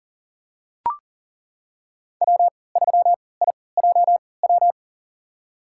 Morse code, heard as a keyed tone, about 40 wpm, spaced as for 23 wpm, 700 Hz. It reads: W3IJW
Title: USA Call Signs - 5 Characters - 3x Character Spacing - 40wpm